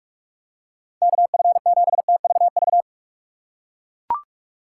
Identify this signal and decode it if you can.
Morse code, heard as a keyed tone, about 42 wpm, 700 Hz. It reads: KF7TVV
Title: USA Call Signs - 6 Characters - 4x Character Spacing - 45wpm